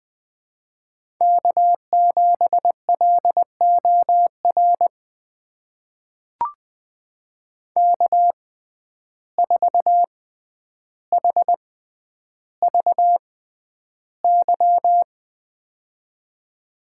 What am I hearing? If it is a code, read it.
Morse code, heard as a keyed tone, about 20 wpm, spaced as for 10 wpm, 700 Hz. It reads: K7LOR K4HVY
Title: USA Call Signs - 5 Characters - 6x Character Spacing - 20wpm